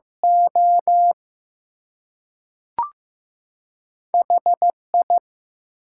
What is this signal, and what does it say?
Morse code, heard as a keyed tone, about 15 wpm, spaced as for 3 wpm, 700 Hz. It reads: OH I
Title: Two Letter Words - 15wpm